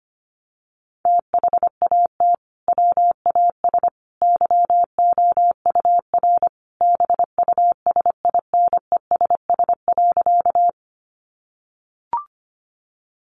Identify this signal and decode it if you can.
Morse code, heard as a keyed tone, about 25 wpm, 700 Hz. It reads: THAT WAS YOUR BUSINESS.
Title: Sentences from Top 600 Words 25wpm